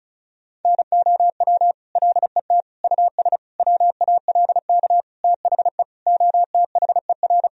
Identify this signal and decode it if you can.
Morse code, heard as a keyed tone, about 34 wpm, 700 Hz. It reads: NOW LET US WALK THE OTHER
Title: Sentences from Top 1000 Words - Review Part 1 - 35wpm